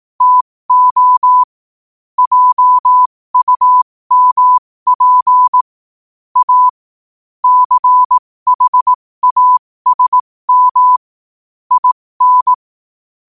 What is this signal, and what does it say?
Morse code, heard as a keyed tone, about 18 wpm, spaced as for 15 wpm, 1000 Hz. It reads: TO JUMP A CHASM IN
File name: qotdfiles / 15wpm